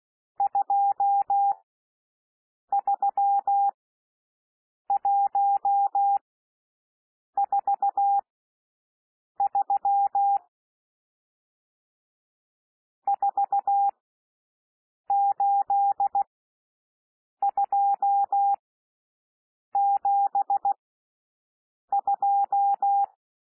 Morse code, heard as a keyed tone, about 16 wpm, spaced as for 6 wpm, 800 Hz. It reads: 23143 48272
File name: SOUND NUMBERS 16 wpm